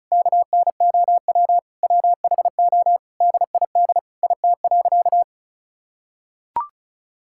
Morse code, heard as a keyed tone, about 36 wpm, 700 Hz. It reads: KNOW WHO DID IT.
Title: Sentences from Top 1000 Words - Review Part 6 - 35wpm